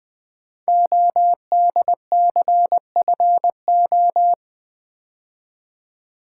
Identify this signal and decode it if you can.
Morse code, heard as a keyed tone, about 20 wpm, 700 Hz. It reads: ODCFO
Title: Code Groups - 5 Letters - 20wpm